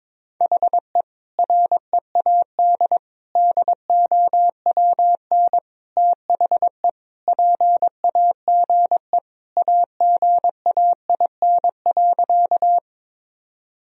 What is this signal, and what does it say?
Morse code, heard as a keyed tone, about 21 wpm, 700 Hz. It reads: HE READ DOWN THE PAGE AGAIN.
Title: Sentences from Top 1000 Words - Review Part 1 - 22wpm